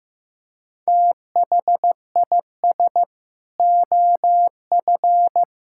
Morse code, heard as a keyed tone, about 15 wpm, 700 Hz. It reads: THIS OF
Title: Sentences from Top 1000 Words - Review Part 5 - 15wpm